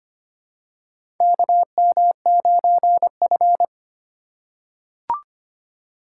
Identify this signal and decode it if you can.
Morse code, heard as a keyed tone, about 25 wpm, 700 Hz. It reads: KM9F
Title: USA Call Signs - 4 Characters - 4x Character Spacing - 25wpm